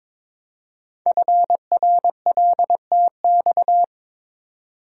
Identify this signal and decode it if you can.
Morse code, heard as a keyed tone, about 22 wpm, 700 Hz. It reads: FRLTX
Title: Code Groups - 5 Letters - 22wpm